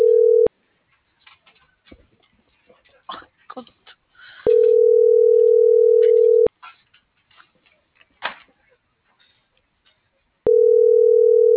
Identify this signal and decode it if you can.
Ringback tone (North America), 440 + 480 Hz.